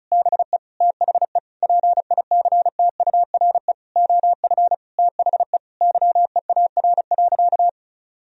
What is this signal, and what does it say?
Morse code, heard as a keyed tone, about 36 wpm, 700 Hz. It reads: BE THE PICTURE OF THE YEAR.
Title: Sentences from Top 1000 Words - Review Part 1 - 35wpm